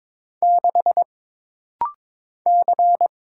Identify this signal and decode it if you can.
Morse code, heard as a keyed tone, about 22 wpm, 700 Hz. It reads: 6 C
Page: This rapid-fire timing is geared to push your ICR (Instant Character Recognition) skills.